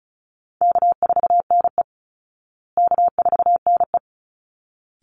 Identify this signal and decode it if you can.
Morse code, heard as a keyed tone, about 34 wpm, 700 Hz. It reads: K4NE K4NE
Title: Easy Call Signs Repeated 2x - 35wpm